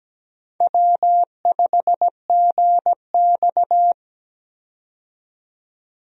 Morse code, heard as a keyed tone, about 17 wpm, 700 Hz. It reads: W5GX
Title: USA Call Signs - 4 Characters - 17wpm